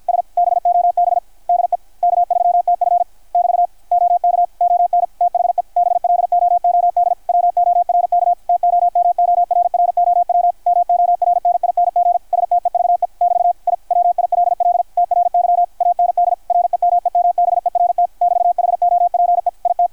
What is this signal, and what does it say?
Morse code, heard as a keyed tone, about 53 wpm, 660 Hz, with radio noise. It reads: S BOB DE K3TF <BT> OK ON THE BLOOD WORK TOMORROW MORNING STEVE <BT> I WILL TRY AND REMEMBERT <BT> HOPE IT
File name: archived / CRSnet